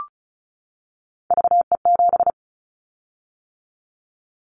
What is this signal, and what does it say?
Morse code, heard as a keyed tone, about 34 wpm, 700 Hz. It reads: VE7
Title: Callsign Prefixes for Top 100 DXCC - 35wpm